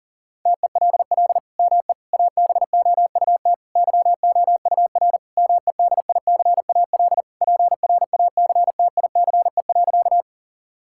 Morse code, heard as a keyed tone, about 40 wpm, 700 Hz. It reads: TELL ME ABOUT YOUR MEDICAL PRACTICE.
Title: Sentences from Top 1000 Words - Review Part 5 - 40wpm